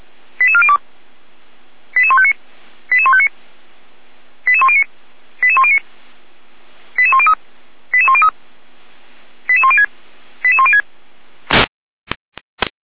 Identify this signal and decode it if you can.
Five-tone selective call (ZVEI), heard as quick runs of five tones, tones of about 70 ms. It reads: 88402 88159 88159 88109 88109 88103 88103 88106 88106